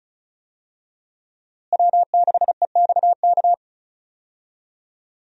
Morse code, heard as a keyed tone, about 34 wpm, 700 Hz. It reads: W6EXK
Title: USA Call Signs - 5 Characters - Repeated 2x - 35wpm